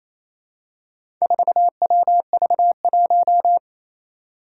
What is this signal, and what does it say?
Morse code, heard as a keyed tone, about 27 wpm, 700 Hz. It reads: 4WV1
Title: Four Challenging Letter-Number Combinations 28wpm